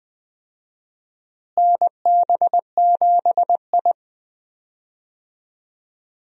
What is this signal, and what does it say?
Morse code, heard as a keyed tone, about 20 wpm, 700 Hz. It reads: NB7I
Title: USA Call Signs - 4 Characters - Repeated 3x - 20wpm